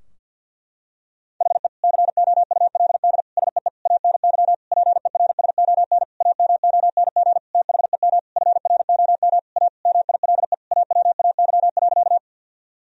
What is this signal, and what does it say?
Morse code, heard as a keyed tone, about 51 wpm, 700 Hz. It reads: HE COULD SEE ANY PERSON AMONG THEM FROM A MILE AWAY.
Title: Sentences from Top 1000 Words - Review Part 3 - 50wpm